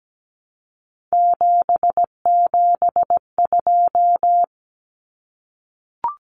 Morse code, heard as a keyed tone, about 17 wpm, 700 Hz. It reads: 772
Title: Numbers - 3-Digit 17wpm